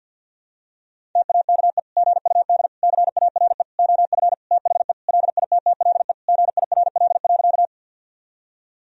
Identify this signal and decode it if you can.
Morse code, heard as a keyed tone, about 49 wpm, 700 Hz. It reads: TAKE GUD CARE OF THE LITTLE GIRL.
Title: Sentences from Top 400 Words and Abbreviations 50wpm